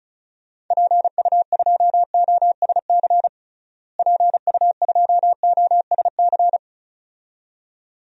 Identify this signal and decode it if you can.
Morse code, heard as a keyed tone, about 34 wpm, 700 Hz. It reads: PU2OSC PU2OSC
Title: DXCC-100 Call Signs - 6 Characters - Repeated 2x - 35wpm